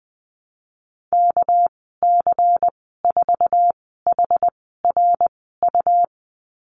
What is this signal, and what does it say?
Morse code, heard as a keyed tone, about 20 wpm, spaced as for 15 wpm, 700 Hz. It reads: KC4HRU
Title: 219 Lesson 35g - Number 0 - Call Signs 20wpm